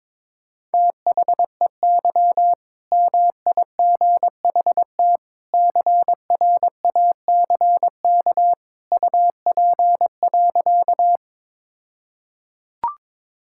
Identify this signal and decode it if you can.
Morse code, heard as a keyed tone, about 22 wpm, 690 Hz. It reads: THEY MIGHT CRACK UP.
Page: morse-code-ninja-practice-22wpm